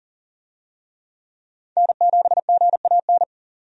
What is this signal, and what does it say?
Morse code, heard as a keyed tone, about 40 wpm, 700 Hz. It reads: N7GAN
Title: USA Call Signs - 5 Characters - 40wpm